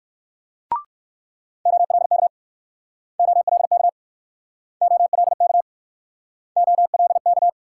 Morse code, heard as a keyed tone, about 65 wpm, 700 Hz. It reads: OLK OLK OLK OLK